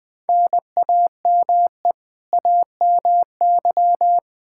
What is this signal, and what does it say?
Morse code, heard as a keyed tone, about 20 wpm, 700 Hz. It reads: NAME AMY
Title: Practice QSO - Exchange 1 - Beginning - 20wpm